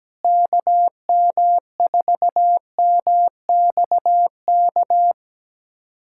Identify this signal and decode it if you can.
Morse code, heard as a keyed tone, about 17 wpm, 700 Hz. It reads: KM4MXK
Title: USA Callsign DE Callsign - 6 Characters - 17wpm